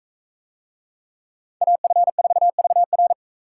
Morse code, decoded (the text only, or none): AF4VR